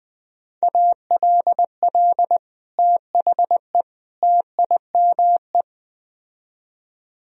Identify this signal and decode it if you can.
Morse code, heard as a keyed tone, about 20 wpm, 700 Hz. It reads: ALL THE TIME